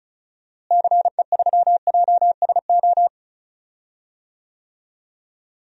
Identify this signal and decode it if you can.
Morse code, heard as a keyed tone, about 34 wpm, 700 Hz. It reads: CE3JSO
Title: DXCC-100 Call Signs - 6 Characters - Repeated 2x - 35wpm